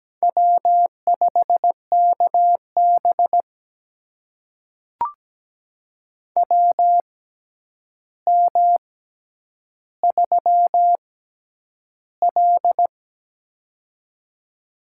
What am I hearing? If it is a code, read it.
Morse code, heard as a keyed tone, about 17 wpm, spaced as for 9 wpm, 700 Hz. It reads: W5KB WM3L
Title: USA Call Signs - 4 Characters - 6x Character Spacing - 17wpm